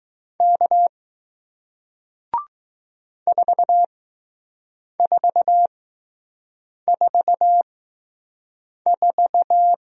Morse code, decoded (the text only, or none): K 4444